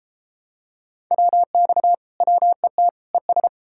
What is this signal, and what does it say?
Morse code, heard as a keyed tone, about 33 wpm, 700 Hz. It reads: WX WET ES